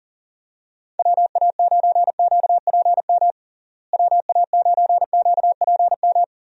Morse code, heard as a keyed tone, about 40 wpm, 700 Hz. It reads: WA9QPM WA9QPM
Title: USA Call Signs - 6 Characters - Repeated 3x - 40wpm